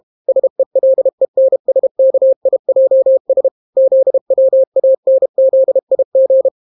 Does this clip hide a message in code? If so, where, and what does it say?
Yes: Morse code from 0.3 to 6.5 s, heard as a keyed tone, about 32 wpm, 530 Hz. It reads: SELENSKIJS ZWANZIG